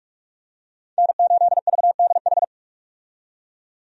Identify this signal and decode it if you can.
Morse code, heard as a keyed tone, about 42 wpm, 700 Hz. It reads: N8VDH
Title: USA Call Signs - 5 Characters - 45wpm